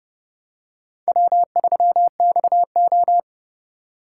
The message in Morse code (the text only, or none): W3XO